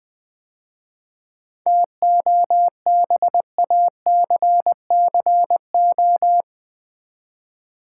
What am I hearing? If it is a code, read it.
Morse code, heard as a keyed tone, about 20 wpm, 700 Hz. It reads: TOBACCO
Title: 7 Letter Words - First 3 Letters Given - 20wpm